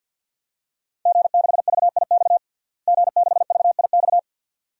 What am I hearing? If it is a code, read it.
Morse code, heard as a keyed tone, about 52 wpm, 700 Hz. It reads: G6VIX G6VIX
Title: DXCC-100 Call Signs - 5 Characters - Repeated 2x - 50wpm